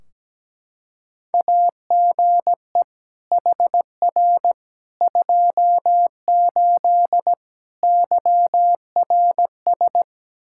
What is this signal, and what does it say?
Morse code, heard as a keyed tone, about 17 wpm, 700 Hz. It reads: AGE HR 28 YRS
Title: Practice QSO - Exchange 3 - More Info - 17wpm